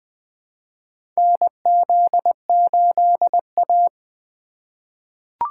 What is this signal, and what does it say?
Morse code, heard as a keyed tone, about 21 wpm, 700 Hz. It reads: NZ8A
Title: USA Call Signs - 4 Characters - 4x Character Spacing - 20wpm